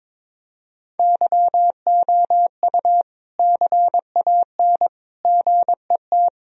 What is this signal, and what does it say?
Morse code, heard as a keyed tone, about 22 wpm, 700 Hz. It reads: YOU CAN GET
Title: Sentences from Top 1000 Words - Review Part 2 - 22wpm